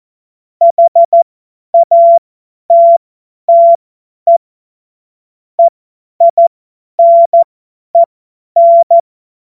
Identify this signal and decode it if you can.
Morse code, heard as a keyed tone, about 14 wpm, spaced as for 10 wpm, 680 Hz. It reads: HATTE EINEN
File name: Morsekurs / Witze / Witz 10. Mai 2022 50 F70